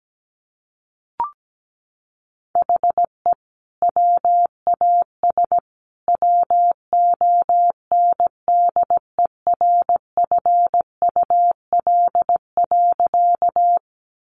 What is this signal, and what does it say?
Morse code, heard as a keyed tone, about 17 wpm, 700 Hz. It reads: HE WAS WONDERFUL.
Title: Sentences from Top 1100 Words 17wpm